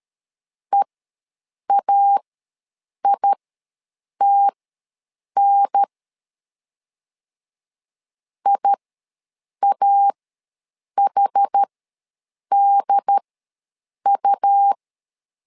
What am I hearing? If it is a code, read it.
Morse code, heard as a keyed tone, about 13 wpm, spaced as for 7 wpm, 800 Hz. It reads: EAITN IAHDU
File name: LEZIONI / 2 UHDN / 2 UHDN ws 12wpm 1708